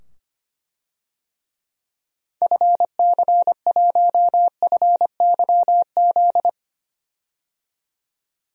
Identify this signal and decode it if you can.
Morse code, heard as a keyed tone, about 25 wpm, 700 Hz. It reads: FC1FYZ
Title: DXCC-100 Call Signs - 6 Characters - 25wpm